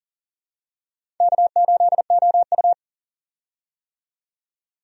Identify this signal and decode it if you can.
Morse code, heard as a keyed tone, about 40 wpm, 700 Hz. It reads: K8OU
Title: USA Call Signs - 4 Characters - 40wpm